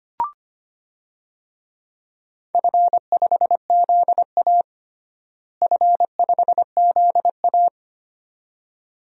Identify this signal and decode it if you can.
Morse code, heard as a keyed tone, about 25 wpm, 700 Hz. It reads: F5ZA F5ZA
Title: DXCC-100 Call Signs - 4 Characters - Repeated 2x - 25wpm